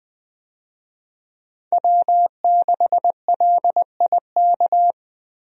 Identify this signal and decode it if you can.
Morse code, heard as a keyed tone, about 20 wpm, 700 Hz. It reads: W6LIK
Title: USA Call Signs - 5 Characters - 20wpm